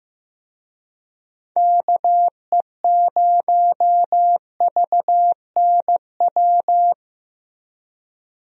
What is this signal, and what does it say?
Morse code, heard as a keyed tone, about 16 wpm, 700 Hz. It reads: KE0VNW
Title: USA Call Signs - 6 Characters - Repeated 3x - 15wpm